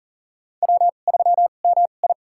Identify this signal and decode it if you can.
Morse code, heard as a keyed tone, about 40 wpm, spaced as for 29 wpm, 700 Hz. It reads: W3MI
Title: USA Call Signs - 2x Character Spacing - 40wpm